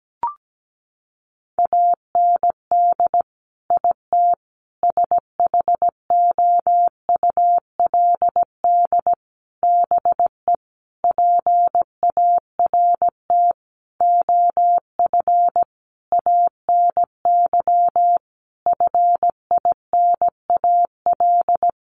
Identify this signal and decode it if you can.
Morse code, heard as a keyed tone, about 17 wpm, 700 Hz. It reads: AND IT SHOULD BE PART OF ANY FINAL
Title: Sentences from Top 1500 Words 17wpm